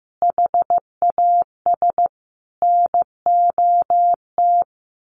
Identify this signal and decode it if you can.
Morse code, heard as a keyed tone, about 15 wpm, 700 Hz. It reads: HAS NOT
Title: A Sound Of Thunder 15wpm